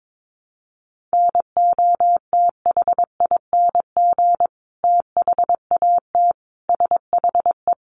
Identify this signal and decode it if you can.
Morse code, heard as a keyed tone, about 22 wpm, 700 Hz. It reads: NOTHING THAT SHE
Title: Sentences from Top 1100 Words 22wpm